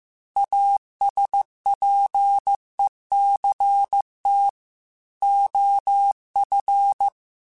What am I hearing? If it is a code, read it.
Morse code, heard as a keyed tone, about 15 wpm, 800 Hz. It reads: ASPECT OF
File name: Technical Subjects / Technical Subjects-mp3s / Sw Pwr Sup @13